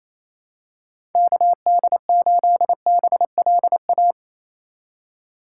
Morse code, heard as a keyed tone, about 28 wpm, 700 Hz. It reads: KD8BLA
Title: USA Call Signs - 2x Character Spacing - 28wpm